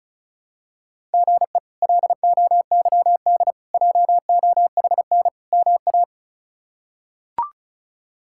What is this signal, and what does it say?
Morse code, heard as a keyed tone, about 34 wpm, 700 Hz. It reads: GE LOYD JOHN MA
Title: K1USN SST QSO Practice - 35wpm